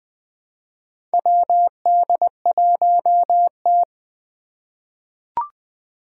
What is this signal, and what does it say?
Morse code, heard as a keyed tone, about 20 wpm, 700 Hz. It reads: WD1T
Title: Four Challenging Letter-Number Combinations 20wpm